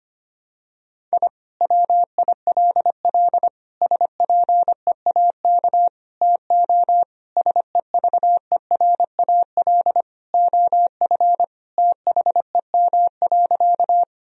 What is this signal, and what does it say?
Morse code, heard as a keyed tone, about 25 wpm, 700 Hz. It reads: I WILL SPEAK TO SEVERAL OF THEM.
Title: Sentences from Top 1000 Words - Review Part 2 - 25wpm